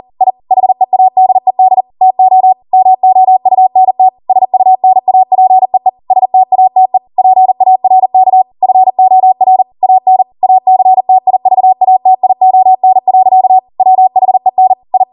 Morse code, heard as a keyed tone, about 40 wpm, 750 Hz. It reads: I HEADED TO MOUNT SUNAPEE STATE PARK FOR AN ACTIVATION. WHEN I